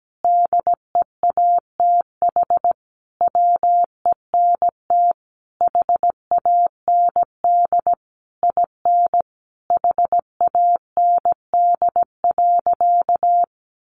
Morse code, heard as a keyed tone, about 17 wpm, 700 Hz. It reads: DEATH WENT HAND IN HAND.
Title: Sentences from Top 1100 Words 17wpm